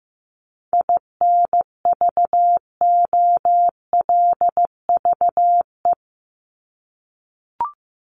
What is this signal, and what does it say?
Morse code, heard as a keyed tone, about 15 wpm, 700 Hz. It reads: INVOLVE
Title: Top 1001-1250 Words 15wpm